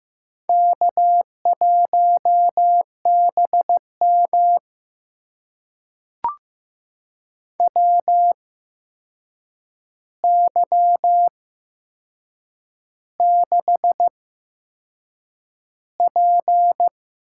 Morse code, heard as a keyed tone, about 15 wpm, spaced as for 6 wpm, 700 Hz. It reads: K1BM WY6P
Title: USA Call Signs - 8x Character Spacing - 15wpm